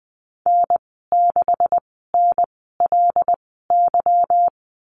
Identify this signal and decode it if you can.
Morse code, heard as a keyed tone, about 20 wpm, spaced as for 15 wpm, 700 Hz. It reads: N6NLY
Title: 167 Lesson 27g - Number 7 - Call Signs 20wpm